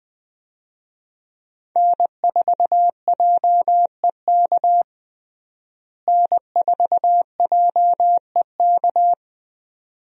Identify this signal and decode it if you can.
Morse code, heard as a keyed tone, about 20 wpm, 700 Hz. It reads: N4JEK N4JEK
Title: USA Call Signs - 5 Characters - Repeated 2x - 20wpm